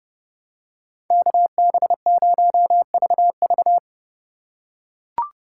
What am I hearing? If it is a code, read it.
Morse code, heard as a keyed tone, about 32 wpm, 700 Hz. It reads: KB0VV
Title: USA Call Signs - 5 Characters - 2x Character Spacing - 30wpm